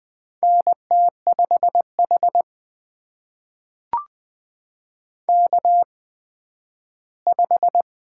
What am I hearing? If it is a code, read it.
Morse code, heard as a keyed tone, about 20 wpm, spaced as for 12 wpm, 700 Hz. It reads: NT5H K5
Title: USA Call Signs - 8x Character Spacing - 20wpm